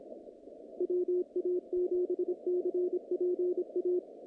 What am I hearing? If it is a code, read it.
Morse code, heard as a keyed tone, about 26 wpm, 360 Hz, with radio noise. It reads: WA7CPA
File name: contests / naqp-cw-k5tr-op-k5pi- -2200